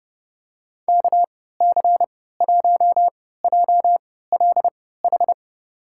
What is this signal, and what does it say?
Morse code, heard as a keyed tone, about 31 wpm, spaced as for 17 wpm, 700 Hz. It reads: KC1JLH
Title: USA Call Signs - 6 Characters - 3x Character Spacing - 30wpm